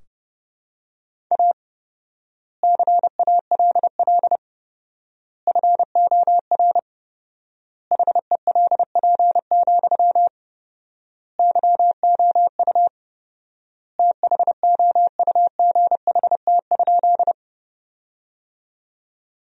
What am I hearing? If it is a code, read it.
Morse code, heard as a keyed tone, about 32 wpm, 700 Hz. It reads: A CALL FOR HELP, YOU THOUGHT?
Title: 4x Word-Spacing - Sentences from Top 300 Words - 30wpm